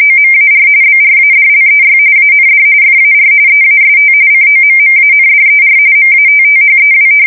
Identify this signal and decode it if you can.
RTTY, about 45.45 baud. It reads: OMAR BRADLEY AND FAR EAST COMMANDER MATTHEW
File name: fsk170 / battleshipnj